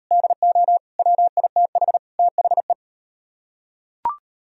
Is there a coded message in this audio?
Yes: Morse code from 0.1 to 2.7 s, heard as a keyed tone, about 38 wpm, 700 Hz. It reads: DO WITH THE